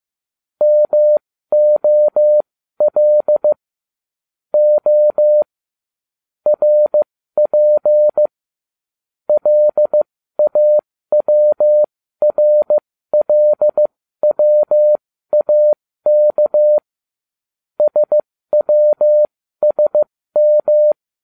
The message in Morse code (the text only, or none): MOL O RP LAWRLWAK SWSM